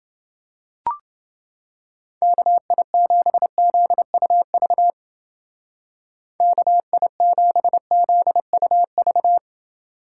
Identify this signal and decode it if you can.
Morse code, heard as a keyed tone, about 30 wpm, 700 Hz. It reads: KI7ZUV KI7ZUV